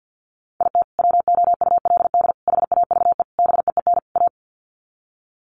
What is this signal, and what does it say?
Morse code, heard as a keyed tone, about 50 wpm, 700 Hz. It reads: IT WOULD HAVE BEEN A
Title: Sets of 5 Words Encore 50wpm